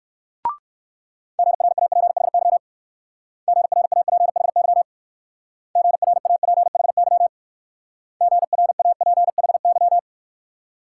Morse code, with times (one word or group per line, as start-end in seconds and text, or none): 1.4-2.6 GRAPHY
3.5-4.8 GRAPHY
5.8-7.3 GRAPHY
8.2-10.0 GRAPHY